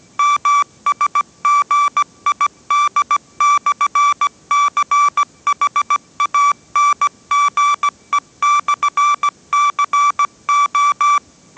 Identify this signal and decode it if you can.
Morse code, heard as a keyed tone, about 17 wpm, 1160 Hz. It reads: MSGID/CHANGE/CO